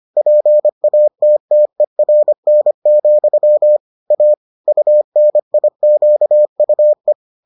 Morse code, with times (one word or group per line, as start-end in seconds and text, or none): 0.2-3.8 PATTERN,
4.1-4.3 A
4.7-7.1 UNIQUE